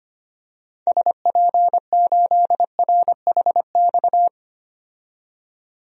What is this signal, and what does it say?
Morse code, heard as a keyed tone, about 25 wpm, 700 Hz. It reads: SP8RHX
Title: DXCC-100 Call Signs - 6 Characters - 25wpm